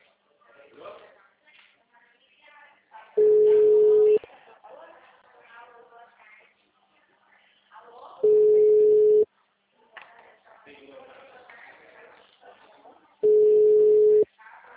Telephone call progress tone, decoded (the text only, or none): ringback tone